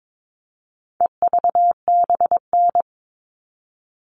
Morse code, heard as a keyed tone, about 22 wpm, 700 Hz. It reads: EVBN
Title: All Cut Numbers - 4-Digit 22wpm